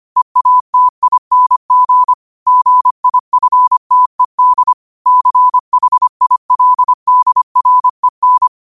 Morse code, heard as a keyed tone, about 25 wpm, 1000 Hz. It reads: EATING GIFTED CHILDREN
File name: qotdfiles / 25wpm